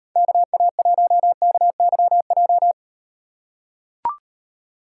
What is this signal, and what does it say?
Morse code, heard as a keyed tone, about 38 wpm, 700 Hz. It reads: KA1KYJ